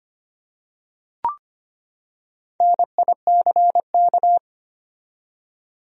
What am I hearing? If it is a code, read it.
Morse code, heard as a keyed tone, about 25 wpm, 700 Hz. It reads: NICK
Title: USA Common Names - Max 5 Letters - 25wpm